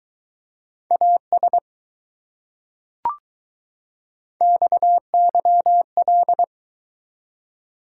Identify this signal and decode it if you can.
Morse code, heard as a keyed tone, about 20 wpm, 700 Hz. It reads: AS XYL